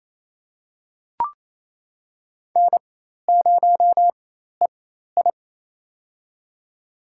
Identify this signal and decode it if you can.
Morse code, heard as a keyed tone, about 28 wpm, spaced as for 13 wpm, 700 Hz. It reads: N0EI
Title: USA Call Signs - 4x Character Spacing - 28wpm